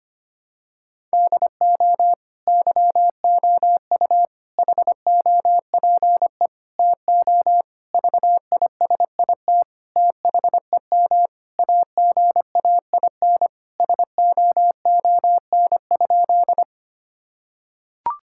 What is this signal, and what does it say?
Morse code, heard as a keyed tone, about 25 wpm, 700 Hz. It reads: DO YOU HOPE TO VISIT THEM AGAIN SOON?
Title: Sentences from Top 1000 Words - Review Part 6 - 25wpm